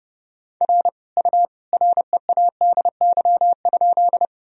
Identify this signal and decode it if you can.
Morse code, heard as a keyed tone, about 31 wpm, 700 Hz. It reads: R U READY?
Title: Sentences from Top 400 Words and Abbreviations 30wpm